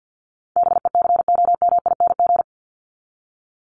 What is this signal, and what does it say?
Morse code, heard as a keyed tone, about 48 wpm, 700 Hz. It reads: BECOMING
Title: ING Suffix - Top 100 Words with Base Words 50wpm